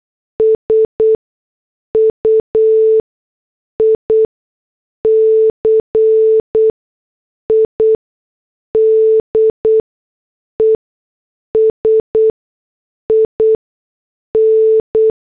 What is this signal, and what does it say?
Morse code, heard as a keyed tone, about 8 wpm, spaced as for 6 wpm, 440 Hz. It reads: SUICIDES IN
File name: morse / headlines